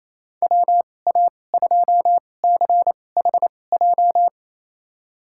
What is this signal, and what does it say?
Morse code, heard as a keyed tone, about 28 wpm, spaced as for 20 wpm, 700 Hz. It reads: WA2CHJ
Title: USA Call Signs - 6 Characters - 2x Character Spacing - 28wpm